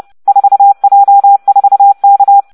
Morse code, heard as a keyed tone, about 30 wpm, 800 Hz. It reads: 4J4K